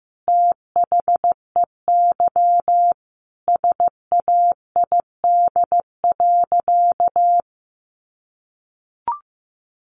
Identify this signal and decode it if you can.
Morse code, heard as a keyed tone, about 15 wpm, 700 Hz. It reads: THEY SAID.
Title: Sentences from Top 100 Words 15wpm